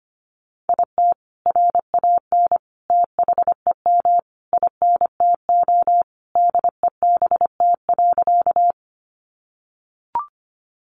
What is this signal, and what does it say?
Morse code, heard as a keyed tone, about 25 wpm, 700 Hz. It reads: IT RAN THEM INTO DEBT.
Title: Sentences from Top 1300 Words 25wpm